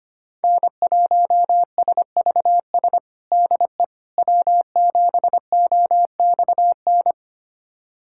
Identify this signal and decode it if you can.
Morse code, heard as a keyed tone, about 25 wpm, 700 Hz. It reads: N1SVS DE W7OXN
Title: USA Callsign DE Callsign - 5 Characters - 25wpm